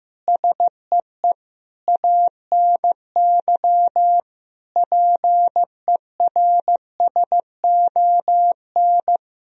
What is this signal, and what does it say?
Morse code, heard as a keyed tone, about 15 wpm, 700 Hz. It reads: SEE ANY PERSON
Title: Sentences from Top 1000 Words - Review Part 3 - 15wpm